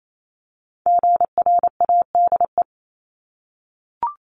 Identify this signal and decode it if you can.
Morse code, heard as a keyed tone, about 28 wpm, 700 Hz. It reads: GRADE
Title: Multi-letter Phonemes 28wpm